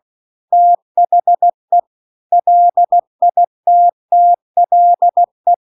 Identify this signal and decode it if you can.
Morse code, heard as a keyed tone, about 16 wpm, 700 Hz. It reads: THE LITTLE
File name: CW-The-Big-Twoheaded-River-2-16WPM